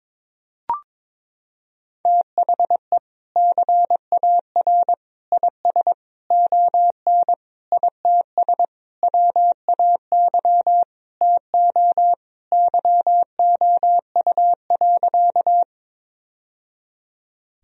Morse code, heard as a keyed tone, about 22 wpm, 700 Hz. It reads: THE CAR IS ON ITS WAY TO YOU.
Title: Sentences from Top 1000 Words - Review Part 1 - 22wpm